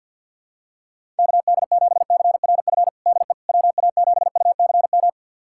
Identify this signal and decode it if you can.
Morse code, heard as a keyed tone, about 51 wpm, 700 Hz. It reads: KD7CRF DE WA7UCM
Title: USA Callsign DE Callsign - 6 Characters - 50wpm